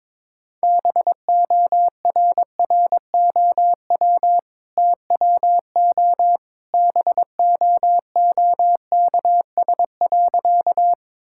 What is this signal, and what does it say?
Morse code, heard as a keyed tone, about 21 wpm, 700 Hz. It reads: BORROW TWO BOOKS.
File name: practice / 22wpm / Easy Sentences with Hints - 22wpm